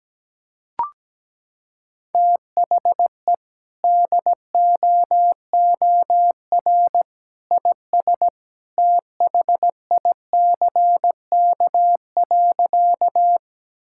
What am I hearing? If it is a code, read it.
Morse code, heard as a keyed tone, about 17 wpm, 700 Hz. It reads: THE DOOR IS THICK.
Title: Sentences from Top 1600 Words 17wpm